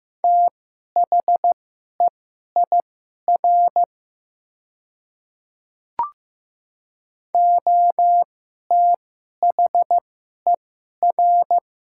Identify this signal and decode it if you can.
Morse code, heard as a keyed tone, about 15 wpm, spaced as for 11 wpm, 700 Hz. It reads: THEIR OTHER
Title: 2x Character Spacing - Top 300 Words - 15wpm